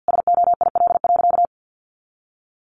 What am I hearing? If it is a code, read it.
Morse code, heard as a keyed tone, about 50 wpm, 700 Hz. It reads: SOIL.
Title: Easy Phrases 50wpm